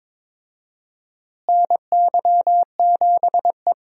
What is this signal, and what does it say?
Morse code, heard as a keyed tone, about 22 wpm, 700 Hz. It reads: NY7E
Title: USA Call Signs - 4 Characters - Repeated 2x - 22wpm